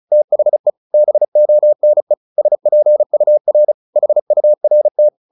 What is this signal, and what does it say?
Morse code, heard as a keyed tone, about 34 wpm, 600 Hz. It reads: THE BONE SPUR HURT
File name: the old man and the sea / the old man and the sea 1 35wpm